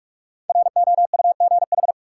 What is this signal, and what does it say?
Morse code, heard as a keyed tone, about 46 wpm, 700 Hz. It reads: ROUGH
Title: Five Letter Words - 45wpm